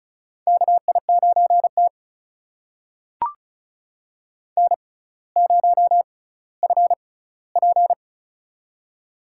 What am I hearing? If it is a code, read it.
Morse code, heard as a keyed tone, about 36 wpm, spaced as for 18 wpm, 700 Hz. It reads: KI9T N0FP
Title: USA Call Signs - 4 Characters - 6x Character Spacing - 35wpm